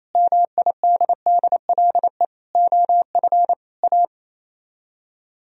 Morse code, heard as a keyed tone, about 33 wpm, 700 Hz. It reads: MIDDLE OF A
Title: Speed-Racing - Sets of 5 Words 28wpm